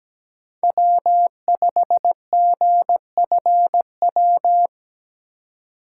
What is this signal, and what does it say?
Morse code, heard as a keyed tone, about 17 wpm, 700 Hz. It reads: W5GFW
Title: USA Call Signs - 2x Character Spacing - 17wpm